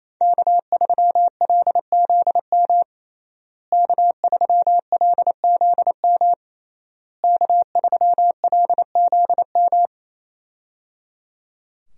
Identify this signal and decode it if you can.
Morse code, heard as a keyed tone, about 28 wpm, 700 Hz. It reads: K3LZM K3LZM K3LZM
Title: USA Call Signs - 5 Characters - Repeated 3x - 28wpm